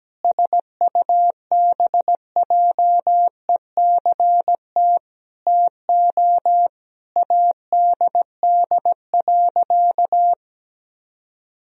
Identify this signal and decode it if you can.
Morse code, heard as a keyed tone, about 17 wpm, 700 Hz. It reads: SUBJECT TO ADD.
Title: Sentences from Top 1000 Words - Review Part 1 - 17wpm